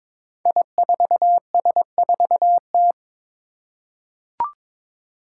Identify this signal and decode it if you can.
Morse code, heard as a keyed tone, about 22 wpm, 700 Hz. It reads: I4S4T
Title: Five Challenging Letter-Number Combinations 22wpm